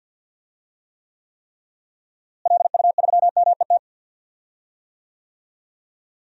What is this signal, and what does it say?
Morse code, heard as a keyed tone, about 52 wpm, 700 Hz. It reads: LU3MET
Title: DXCC-100 Call Signs - 6 Characters - Repeated 3x - 50wpm